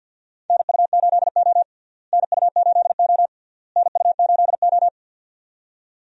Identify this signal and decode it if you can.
Morse code, heard as a keyed tone, about 46 wpm, 700 Hz. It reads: NU8O NU8O NU8O
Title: USA Call Signs - 4 Characters - Repeated 3x - 50wpm